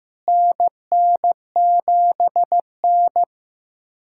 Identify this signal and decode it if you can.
Morse code, heard as a keyed tone, about 15 wpm, 700 Hz. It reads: NN7N
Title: USA Callsign DE Callsign - 4 Characters - 15wpm